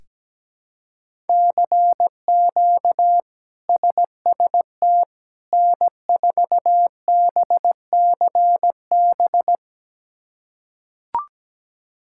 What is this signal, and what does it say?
Morse code, heard as a keyed tone, about 17 wpm, 700 Hz. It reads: CQ SST N4BCB
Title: K1USN SST QSO Practice - 17wpm